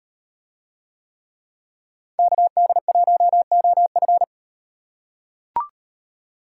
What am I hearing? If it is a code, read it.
Morse code, heard as a keyed tone, about 37 wpm, 700 Hz. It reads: KD1OF